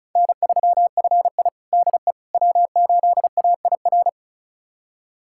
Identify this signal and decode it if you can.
Morse code, heard as a keyed tone, about 36 wpm, 700 Hz. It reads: N3FI DE W8AIR
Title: USA Callsign DE Callsign 35wpm